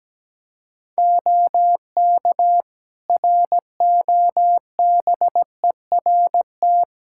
Morse code, heard as a keyed tone, about 17 wpm, 700 Hz. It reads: OK ROBERT
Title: Practice QSO - Exchange 4 - Ending - 17wpm